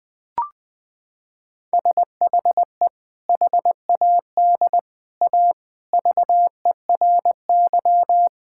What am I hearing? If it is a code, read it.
Morse code, heard as a keyed tone, about 20 wpm, 700 Hz. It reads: SHE HAD A VERY
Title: Sentences with 2-Word Phrases - 20wpm